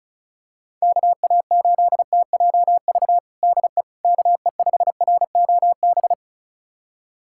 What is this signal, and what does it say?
Morse code, heard as a keyed tone, about 36 wpm, 700 Hz. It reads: KA8TJV DE KE5ROB
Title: USA Callsign DE Callsign 35wpm